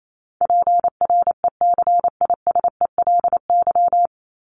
Morse code, heard as a keyed tone, about 28 wpm, 700 Hz. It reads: PRECISELY
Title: LY Suffix - Top 100 Words with Base Words 28wpm